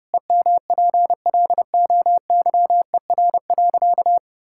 Morse code, heard as a keyed tone, about 32 wpm, 700 Hz. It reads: EMPLOYER.
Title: Sentences from Top 1600 Words 30wpm